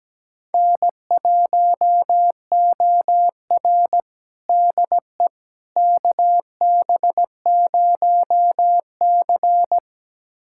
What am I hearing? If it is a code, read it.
Morse code, heard as a keyed tone, about 17 wpm, 700 Hz. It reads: N1OR DE KB0C
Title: USA Callsign DE Callsign - 4 Characters - 17wpm